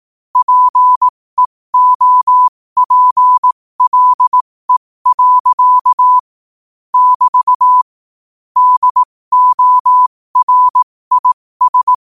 Morse code, heard as a keyed tone, about 18 wpm, spaced as for 15 wpm, 1000 Hz. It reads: PEOPLE. <BT> DORIS
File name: qotdfiles / 15wpm